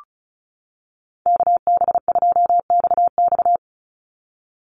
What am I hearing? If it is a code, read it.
Morse code, heard as a keyed tone, about 34 wpm, 700 Hz. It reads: KB2XX
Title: NA SOTA Chaser Callsigns 35wpm